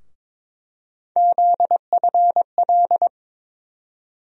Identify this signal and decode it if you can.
Morse code, heard as a keyed tone, about 22 wpm, 700 Hz. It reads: ZFL
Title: USA Call Sign Postfixes - 3 Letters 22wpm